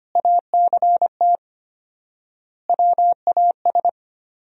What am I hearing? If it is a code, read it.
Morse code, heard as a keyed tone, about 25 wpm, 700 Hz. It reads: ACT WAS
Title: 4x Word-Spacing - Sentences from Top 200 Words - 25wpm